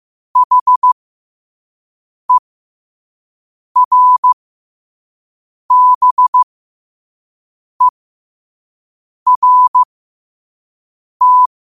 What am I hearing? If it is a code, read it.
Morse code, heard as a keyed tone, about 15 wpm, spaced as for 5 wpm, 1000 Hz. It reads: HERBERT
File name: qotdfiles / 5wpm